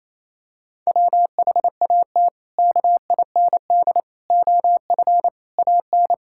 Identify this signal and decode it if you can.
Morse code, heard as a keyed tone, about 27 wpm, 700 Hz. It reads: WHAT KIND OF AN
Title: Sentences from Top 1000 Words - Review Part 5 - 28wpm